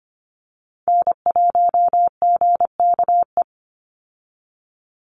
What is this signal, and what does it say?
Morse code, heard as a keyed tone, about 25 wpm, 700 Hz. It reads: N1GKE
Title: Call Signs Repeated 3x - 25wpm